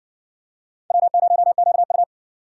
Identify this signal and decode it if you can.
Morse code, heard as a keyed tone, about 60 wpm, 700 Hz. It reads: W0QU